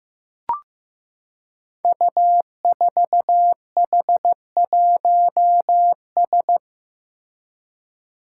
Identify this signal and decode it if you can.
Morse code, heard as a keyed tone, about 15 wpm, 700 Hz. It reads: U4H1S
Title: Five Challenging Letter-Number Combinations 15wpm